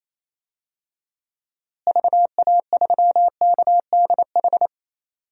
Morse code, heard as a keyed tone, about 28 wpm, 700 Hz. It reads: VA3KDH
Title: DXCC-100 Call Signs - 6 Characters - Repeated 2x - 28wpm